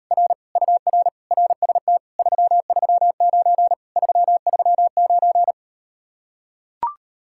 Morse code, heard as a keyed tone, about 38 wpm, 700 Hz. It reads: R UR RST 339 339